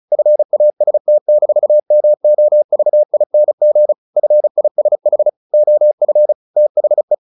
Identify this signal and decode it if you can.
Morse code, heard as a keyed tone, about 34 wpm, 600 Hz. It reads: FAST-MOVING FISH OF THE
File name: the old man and the sea 2 35wpm